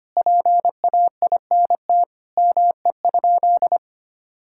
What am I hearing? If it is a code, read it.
Morse code, heard as a keyed tone, about 25 wpm, 700 Hz. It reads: PAINT ME?
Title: Sentences from Top 1000 Words - Review Part 6 - 25wpm